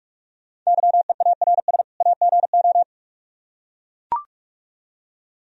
Morse code, heard as a keyed tone, about 44 wpm, 700 Hz. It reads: YEARS AGO